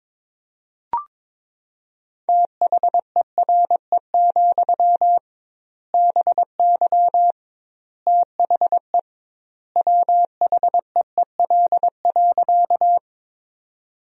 Morse code, heard as a keyed tone, about 21 wpm, 700 Hz. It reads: THERE, BY THE WHEEL.
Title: 2x Word-Spacing - Sentences from Top 500 Words - 22wpm